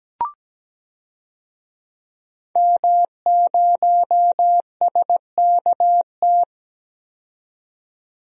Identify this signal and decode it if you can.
Morse code, heard as a keyed tone, about 17 wpm, 700 Hz. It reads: M0SKT
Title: DXCC-100 Call Signs - 5 Characters - 17wpm